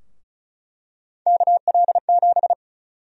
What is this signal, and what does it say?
Morse code, heard as a keyed tone, about 34 wpm, 700 Hz. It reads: KL7
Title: USA Call Sign Prefixes - Repeated 2x - 35wpm